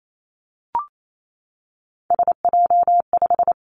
Morse code, heard as a keyed tone, about 28 wpm, 700 Hz. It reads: SJ5
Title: Three Challenging Letter-Number Combinations 28wpm